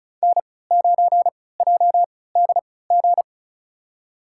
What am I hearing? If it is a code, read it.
Morse code, heard as a keyed tone, about 36 wpm, spaced as for 20 wpm, 700 Hz. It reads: N9JDG
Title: USA Call Signs - 3x Character Spacing - 35wpm